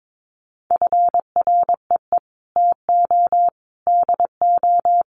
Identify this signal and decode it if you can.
Morse code, heard as a keyed tone, about 22 wpm, 700 Hz. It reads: FREE TO DO
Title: Sentences from Top 500 Words 22wpm